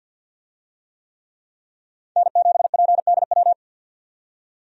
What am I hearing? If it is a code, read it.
Morse code, heard as a keyed tone, about 52 wpm, 700 Hz. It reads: N7PDW
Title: USA Call Signs - 5 Characters - 50wpm